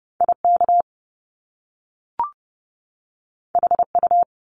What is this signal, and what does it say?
Morse code, heard as a keyed tone, about 30 wpm, 700 Hz. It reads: IK HU
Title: Top Two Letter Combinations in Words 30wpm